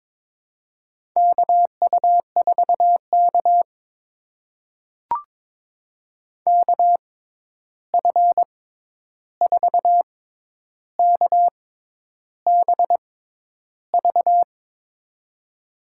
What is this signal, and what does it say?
Morse code, heard as a keyed tone, about 22 wpm, spaced as for 10 wpm, 700 Hz. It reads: KU4K KF4KBV
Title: USA Call Signs - 6x Character Spacing - 22wpm